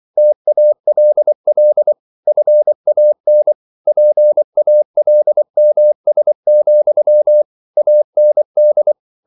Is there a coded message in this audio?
Yes: Morse code from 0.2 to 8.9 s, heard as a keyed tone, about 24 wpm, 600 Hz. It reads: TALL FAN PALMS, AND